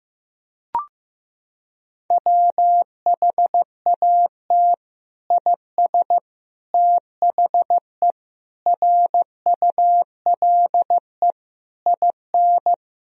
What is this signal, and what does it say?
Morse code, heard as a keyed tone, about 15 wpm, 700 Hz. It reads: WHAT IS THE RULE IN
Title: Sentences from Top 1000 Words - Review Part 2 - 15wpm